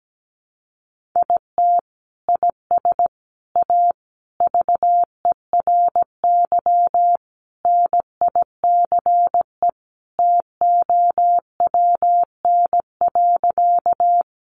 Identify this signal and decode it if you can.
Morse code, heard as a keyed tone, about 17 wpm, 700 Hz. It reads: IT IS A VERY NICE TOWN.
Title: Sentences from Top 1100 Words 17wpm